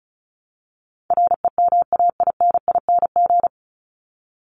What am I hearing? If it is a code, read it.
Morse code, heard as a keyed tone, about 34 wpm, 700 Hz. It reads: REMAINING
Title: ING Suffix - Top 100 Words with Base Words 35wpm